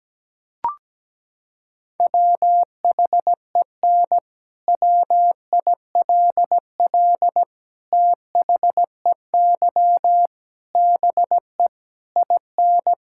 Morse code, heard as a keyed tone, about 17 wpm, 700 Hz. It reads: WHEN WILL THEY BE IN
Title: Sentences from Top 1000 Words - Review Part 6 - 17wpm